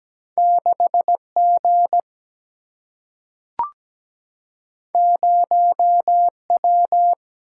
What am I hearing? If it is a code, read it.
Morse code, heard as a keyed tone, about 17 wpm, 700 Hz. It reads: 6G 0W
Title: Single-Letter Single-Number 17wpm